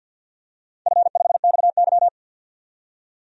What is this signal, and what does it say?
Morse code, heard as a keyed tone, about 47 wpm, 700 Hz. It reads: F5XY
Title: DXCC-100 Call Signs - 4 Characters - 50wpm